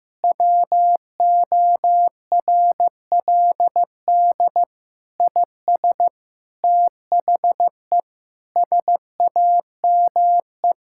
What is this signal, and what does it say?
Morse code, heard as a keyed tone, about 15 wpm, 700 Hz. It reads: WORLD IS THE SAME
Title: Sentences from Top 1600 Words 15wpm